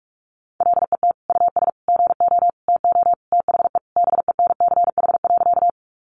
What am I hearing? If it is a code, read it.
Morse code, heard as a keyed tone, about 44 wpm, 700 Hz. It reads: LET US GO TO THE BENCH.
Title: Sentences from Top 1400 Words 45wpm